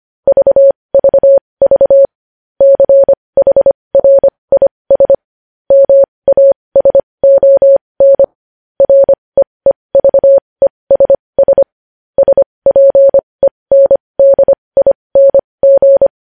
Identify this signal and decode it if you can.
Morse code, heard as a keyed tone, about 25 wpm, spaced as for 20 wpm, 550 Hz. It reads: VVV CHRIS MASON REEVESS SPENDING
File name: podcast / News headlines-1x-25x20